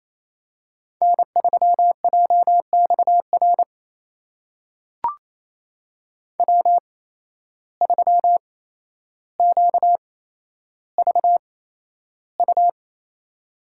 Morse code, heard as a keyed tone, about 28 wpm, spaced as for 12 wpm, 700 Hz. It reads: N3JXR W3QVU
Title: USA Call Signs - 5 Characters - 8x Character Spacing - 28wpm